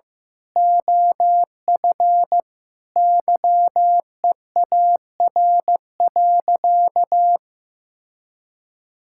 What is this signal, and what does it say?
Morse code, heard as a keyed tone, about 15 wpm, 700 Hz. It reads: OF YEAR.
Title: Sentences from Top 1000 Words - Review Part 6 - 15wpm